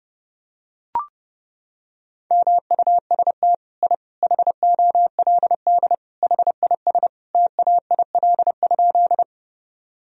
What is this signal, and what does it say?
Morse code, heard as a keyed tone, about 32 wpm, 700 Hz. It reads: MUST I HOLD HIS TAIL?
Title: Sentences from Top 1000 Words - Review Part 3 - 30wpm